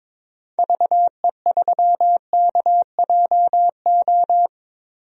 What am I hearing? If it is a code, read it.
Morse code, heard as a keyed tone, about 22 wpm, 700 Hz. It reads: VE3KJO
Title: DXCC-100 Call Signs - 6 Characters - Repeated 2x - 22wpm